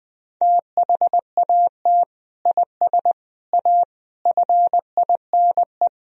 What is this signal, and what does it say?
Morse code, heard as a keyed tone, about 20 wpm, 700 Hz. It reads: THAT IS A FINE
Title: Sentences from Top 1000 Words - Review Part 4 - 20wpm